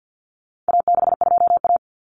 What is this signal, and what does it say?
Morse code, heard as a keyed tone, about 50 wpm, 700 Hz. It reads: A62A
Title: Common Cut Numbers - 4-Digit 50wpm